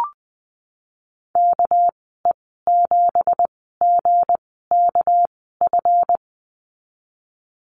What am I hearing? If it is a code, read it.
Morse code, heard as a keyed tone, about 20 wpm, spaced as for 15 wpm, 700 Hz. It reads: KE7GKF
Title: 213 Lesson 34g - Number 8 - Call Signs 20wpm